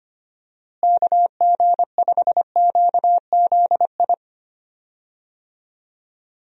Morse code, heard as a keyed tone, about 25 wpm, 700 Hz. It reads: KG5QZI
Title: USA Call Signs - 6 Characters - Repeated 3x - 25wpm